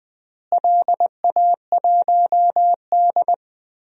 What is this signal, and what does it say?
Morse code, heard as a keyed tone, about 20 wpm, 700 Hz. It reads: LA1D
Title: DXCC-100 Call Signs - 4 Characters - Repeated 3x - 20wpm